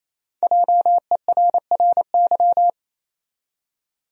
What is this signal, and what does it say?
Morse code, heard as a keyed tone, about 28 wpm, 700 Hz. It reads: JERRY
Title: USA Common Names - 28wpm